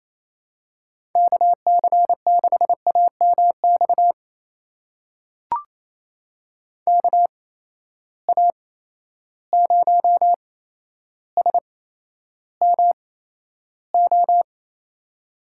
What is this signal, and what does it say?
Morse code, heard as a keyed tone, about 28 wpm, spaced as for 12 wpm, 700 Hz. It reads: KC6AMX KA0SMO
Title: USA Call Signs - 6 Characters - 8x Character Spacing - 28wpm